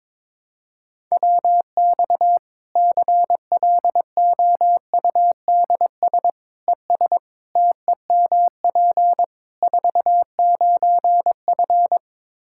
Morse code, heard as a keyed tone, about 22 wpm, 700 Hz. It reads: WX CLOUDS ES TEMP 49F
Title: K1USN SST QSO Practice 22wpm